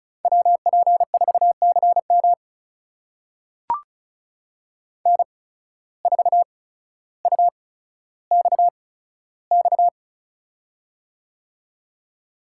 Morse code, heard as a keyed tone, about 36 wpm, spaced as for 15 wpm, 700 Hz. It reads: WP4CM N4UXX
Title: USA Call Signs - 5 Characters - 8x Character Spacing - 35wpm